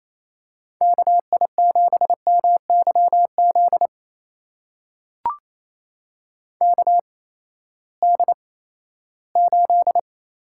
Morse code, heard as a keyed tone, about 28 wpm, spaced as for 16 wpm, 700 Hz. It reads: KI7MYZ KD8
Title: USA Call Signs - 8x Character Spacing - 28wpm